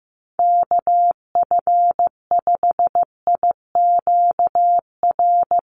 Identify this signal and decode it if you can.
Morse code, heard as a keyed tone, about 15 wpm, 700 Hz. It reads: KF5IQR
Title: USA Call Signs Repeated 3x 15wpm